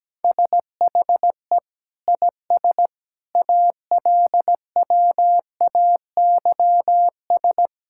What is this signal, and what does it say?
Morse code, heard as a keyed tone, about 17 wpm, 700 Hz. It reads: SHE IS ALWAYS
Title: Sentences from Top 1000 Words - Review Part 5 - 17wpm